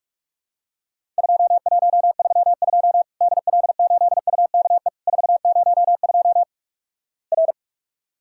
Morse code, heard as a keyed tone, about 46 wpm, 700 Hz. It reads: 2132 DL8UKE 402 R
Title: Worked All Europe DX Contest - QTC Exchange - 45wpm